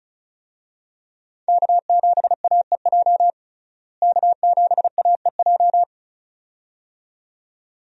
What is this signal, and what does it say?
Morse code, heard as a keyed tone, about 34 wpm, 700 Hz. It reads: K7AEJ K7AEJ
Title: USA Call Signs - 5 Characters - Repeated 2x - 35wpm